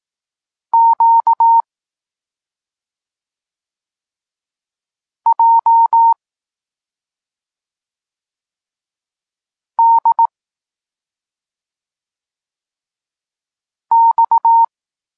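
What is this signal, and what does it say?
Morse code, heard as a keyed tone, about 18 wpm, 920 Hz. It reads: Q J D X
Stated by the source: Polish Morse lesson teaching the letter C, from SP9MOA Niepołomicki Klub Krótkofalowców.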